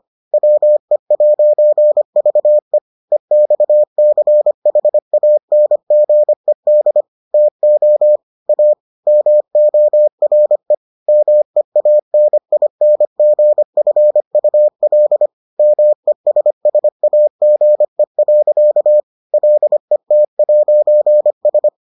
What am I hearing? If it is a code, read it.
Morse code, heard as a keyed tone, about 25 wpm, 600 Hz. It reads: WE'VE EXCHANGED TO A MORE MEANINGFUL MESSAGE. LET'S